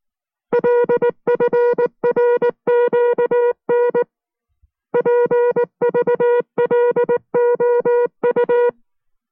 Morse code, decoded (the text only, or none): LFRQN PVLOU